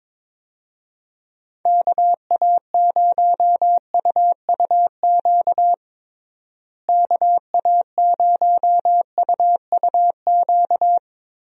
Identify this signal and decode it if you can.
Morse code, heard as a keyed tone, about 22 wpm, 700 Hz. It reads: KA0UUQ KA0UUQ
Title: USA Call Signs - 6 Characters - Repeated 3x - 22wpm